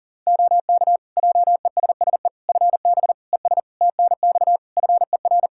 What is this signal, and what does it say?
Morse code, heard as a keyed tone, about 40 wpm, 700 Hz. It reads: OK JESSE FB ES TNX FER
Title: Practice QSO - Exchange 2 - Info - 40wpm